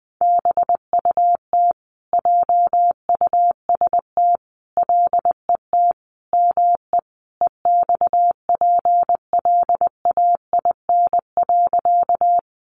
Morse code, heard as a keyed tone, about 19 wpm, 700 Hz. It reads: BUT JUST LET ME EXPLAIN.
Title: Sentences from Top 1100 Words 20wpm